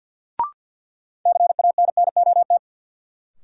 Morse code, heard as a keyed tone, about 52 wpm, 700 Hz. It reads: CANNOT
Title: Sentences from Top 1000 Words - Review Part 6 - New Word Drill - 50wpm